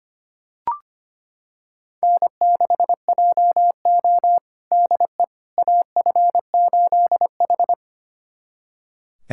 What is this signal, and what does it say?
Morse code, heard as a keyed tone, about 25 wpm, 700 Hz. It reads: N6JO DE AF8H
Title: USA Callsign DE Callsign - 4 Characters - 25wpm